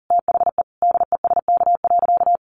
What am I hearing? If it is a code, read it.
Morse code, heard as a keyed tone, about 40 wpm, 700 Hz. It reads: THE DESK.
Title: A Sound Of Thunder 40wpm